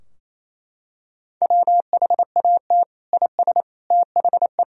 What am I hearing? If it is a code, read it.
Morse code, heard as a keyed tone, about 28 wpm, 700 Hz. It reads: WHAT IS THE
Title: Sentences from Top 1000 Words - Review Part 4 - 28wpm